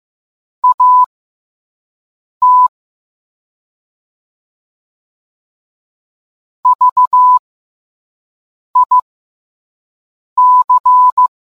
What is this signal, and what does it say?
Morse code, heard as a keyed tone, about 15 wpm, spaced as for 5 wpm, 1000 Hz. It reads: AT VIC